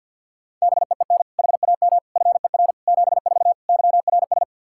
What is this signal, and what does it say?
Morse code, heard as a keyed tone, about 47 wpm, 700 Hz. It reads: BEEN HAM FER 74 YRS
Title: Practice QSO - Exchange 3 - More Info - 50wpm